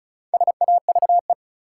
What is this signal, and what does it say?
Morse code, heard as a keyed tone, about 34 wpm, 700 Hz. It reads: SAVE
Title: Four Letter Words - 35wpm